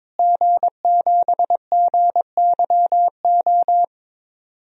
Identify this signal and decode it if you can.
Morse code, heard as a keyed tone, about 22 wpm, 700 Hz. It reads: G7GYO
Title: DXCC-100 Call Signs - 5 Characters - Repeated 2x - 22wpm